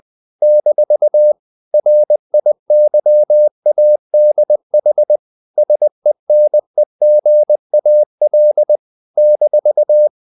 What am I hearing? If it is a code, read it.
Morse code, heard as a keyed tone, about 20 wpm, 600 Hz. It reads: - RIYADH SENEGAL -
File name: CW / Practice / NationalCapitals20